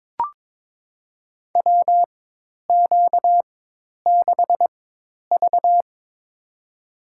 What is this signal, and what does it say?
Morse code, heard as a keyed tone, about 22 wpm, spaced as for 10 wpm, 700 Hz. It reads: WQ6V
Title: USA Call Signs - 4 Characters - 4x Character Spacing - 22wpm